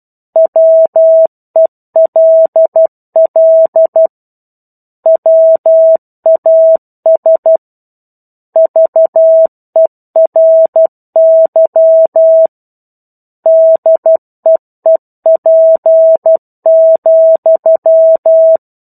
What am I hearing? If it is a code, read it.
Morse code, heard as a keyed tone, about 12 wpm, 650 Hz. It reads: WELL WAS VERY DEEP,
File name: 01-Chapter-01 12 WPM